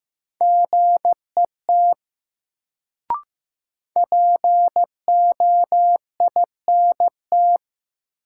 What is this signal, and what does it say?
Morse code, heard as a keyed tone, about 16 wpm, 700 Hz. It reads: GET POINT
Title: Sentences from Top 200 Words - New Word Drill - 15wpm